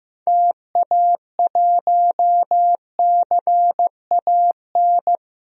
Morse code, heard as a keyed tone, about 15 wpm, 700 Hz. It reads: TA1CAN
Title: DXCC-100 Call Signs - 6 Characters - 15wpm